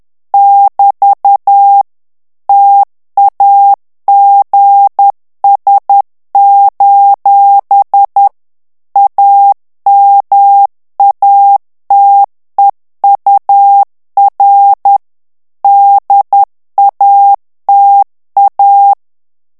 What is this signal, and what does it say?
Morse code, heard as a keyed tone, about 10 wpm, 800 Hz. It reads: <BT> TAGS: AMATEUR DATA
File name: ANGLAIS texte clair 3 vitesse 10 mots minute